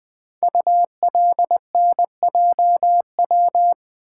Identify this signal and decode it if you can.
Morse code, heard as a keyed tone, about 20 wpm, 700 Hz. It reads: ULNJW
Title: Code Groups - 5 Letters - 20wpm